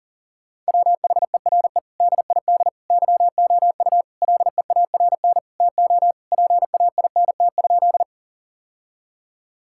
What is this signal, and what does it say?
Morse code, heard as a keyed tone, about 40 wpm, 700 Hz. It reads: WHERE DID YOU LEARN TO PAINT?
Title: Sentences from Top 1000 Words - Review Part 6 - 40wpm